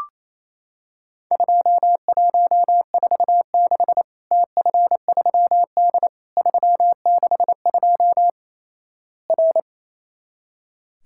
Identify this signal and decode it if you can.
Morse code, heard as a keyed tone, about 28 wpm, 700 Hz. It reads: 2146 TF3D 362 R
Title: Worked All Europe DX Contest - QTC Exchange - 28wpm